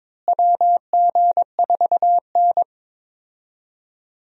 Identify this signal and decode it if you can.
Morse code, heard as a keyed tone, about 22 wpm, 700 Hz. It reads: WG4N
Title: USA Call Signs - 4 Characters - Repeated 3x - 22wpm